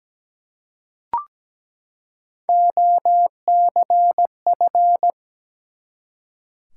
Morse code, heard as a keyed tone, about 17 wpm, 700 Hz. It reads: OCF
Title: Top 180 QSO Elements 17wpm